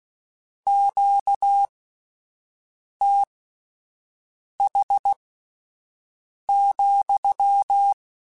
Morse code, heard as a keyed tone, about 16 wpm, spaced as for 5 wpm, 800 Hz. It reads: QTH,
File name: Q Calls / Q Calls-mp3s / Q Calls sppc @10